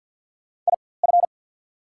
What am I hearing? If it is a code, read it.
Morse code, heard as a keyed tone, about 48 wpm, spaced as for 23 wpm, 700 Hz. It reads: IF
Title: 4x Character Spacing - Top 100 Words - 50wpm